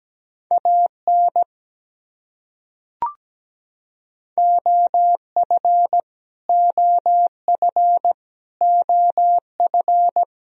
Morse code, heard as a keyed tone, about 17 wpm, 700 Hz. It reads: AN OF OF OF
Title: Top 100 Words Repeated 17wpm